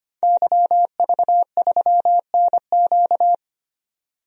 Morse code, heard as a keyed tone, about 25 wpm, 700 Hz. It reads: YV3NQ
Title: DXCC-100 Call Signs - 5 Characters - 25wpm